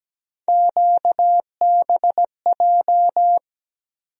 Morse code, heard as a keyed tone, about 17 wpm, 700 Hz. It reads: QBJ
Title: USA Call Sign Postfixes - 3 Letters 17wpm